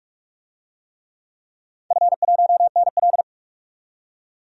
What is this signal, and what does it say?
Morse code, heard as a keyed tone, about 46 wpm, 700 Hz. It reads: F1NL
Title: DXCC-100 Call Signs - 4 Characters - 45wpm